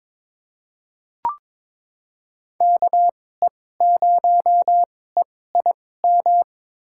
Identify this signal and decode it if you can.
Morse code, heard as a keyed tone, about 22 wpm, spaced as for 16 wpm, 700 Hz. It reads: KE0EIM
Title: USA Call Signs - 2x Character Spacing - 22wpm